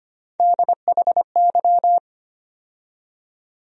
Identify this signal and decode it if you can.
Morse code, heard as a keyed tone, about 25 wpm, 700 Hz. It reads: DHY
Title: USA Call Sign Postfixes - 3 Letters 25wpm